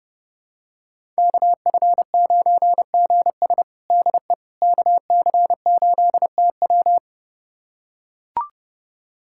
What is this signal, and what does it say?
Morse code, heard as a keyed tone, about 32 wpm, 700 Hz. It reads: KF9GS DE KC8TW
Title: USA Callsign DE Callsign - 5 Characters - 30wpm